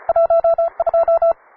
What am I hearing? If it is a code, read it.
Morse code, heard as a keyed tone, about 34 wpm, 670 Hz. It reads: 12